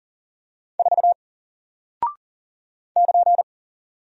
Morse code, heard as a keyed tone, about 40 wpm, 700 Hz. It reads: 4 <KN>
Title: Single Character (Letter:Number:Punctuation:Prosign) Alt 40wpm